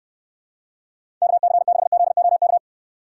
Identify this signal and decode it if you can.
Morse code, heard as a keyed tone, about 69 wpm, 700 Hz. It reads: KC6ZOK